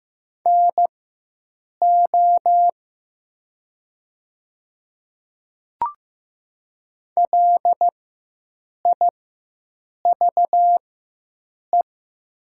Morse code, heard as a keyed tone, about 15 wpm, spaced as for 7 wpm, 700 Hz. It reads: NO LIVE
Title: 4x Character Spacing - Top 200 Words - 15wpm